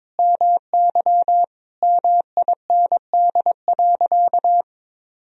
Morse code, heard as a keyed tone, about 21 wpm, 700 Hz. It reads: MY MIND.
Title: Sentences from Top 1000 Words - Review Part 1 - 22wpm